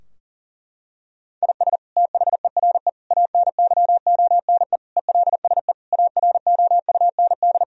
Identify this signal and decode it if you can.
Morse code, heard as a keyed tone, about 40 wpm, 700 Hz. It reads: IS THERE ANYONE ELSE AROUND
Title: Sentences from Top 1000 Words - Review Part 3 - 40wpm